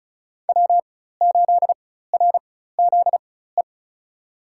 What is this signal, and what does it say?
Morse code, heard as a keyed tone, about 36 wpm, spaced as for 16 wpm, 700 Hz. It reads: W8RZE
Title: USA Call Signs - 5 Characters - 4x Character Spacing - 35wpm